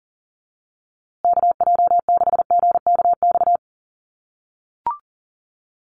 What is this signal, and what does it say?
Morse code, heard as a keyed tone, about 40 wpm, 700 Hz. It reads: KJ6GKX
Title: USA Call Signs Repeated 3x 40wpm